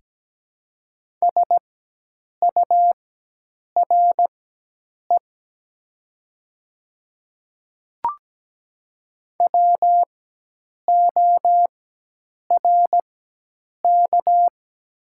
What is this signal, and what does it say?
Morse code, heard as a keyed tone, about 17 wpm, spaced as for 8 wpm, 700 Hz. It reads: SURE WORK
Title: 4x Character Spacing - Top 300 Words - 17wpm